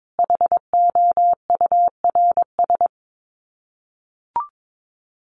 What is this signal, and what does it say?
Morse code, heard as a keyed tone, about 22 wpm, 700 Hz. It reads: HOURS
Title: Sets of 5 Words Encore2 22wpm